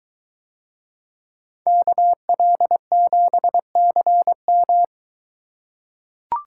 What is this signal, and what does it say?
Morse code, heard as a keyed tone, about 22 wpm, 700 Hz. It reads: KL7CM